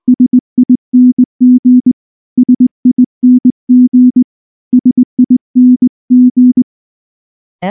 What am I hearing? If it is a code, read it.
Morse code, heard as a keyed tone, about 19 wpm, 250 Hz. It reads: SING SING SING
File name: morse / morse-training-20WPM-2